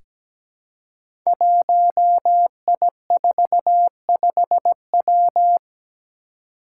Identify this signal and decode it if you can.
Morse code, heard as a keyed tone, about 17 wpm, 700 Hz. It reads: 1I45W
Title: Five Challenging Letter-Number Combinations 17wpm